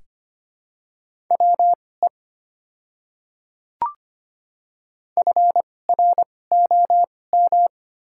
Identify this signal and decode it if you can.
Morse code, heard as a keyed tone, about 25 wpm, spaced as for 18 wpm, 700 Hz. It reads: WE FROM
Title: 2x Character Spacing - Top 100 Words - 25wpm